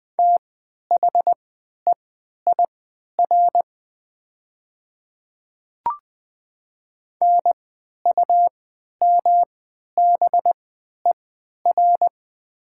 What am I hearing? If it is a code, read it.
Morse code, heard as a keyed tone, about 20 wpm, spaced as for 11 wpm, 700 Hz. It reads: THEIR NUMBER
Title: 3x Character Spacing - Top 200 Words - 20wpm